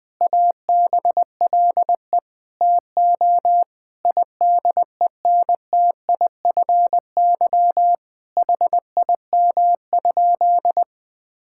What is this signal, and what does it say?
Morse code, heard as a keyed tone, about 20 wpm, 700 Hz. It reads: ABLE TO IDENTIFY HIM?
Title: Sentences from Top 1000 Words - Review Part 6 - 20wpm